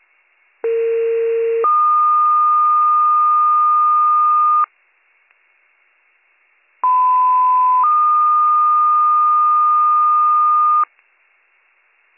Two-tone page, heard as a one-second tone, then a three-second tone, about 460 then 1220 Hz, about 1010 then 1220 Hz. It is heard from 0.6 to 10.8 s.